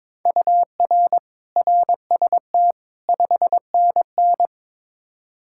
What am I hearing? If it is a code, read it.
Morse code, heard as a keyed tone, about 22 wpm, 700 Hz. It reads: UR RST 5NN
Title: Practice QSO - Exchange 1 - Beginning - 22wpm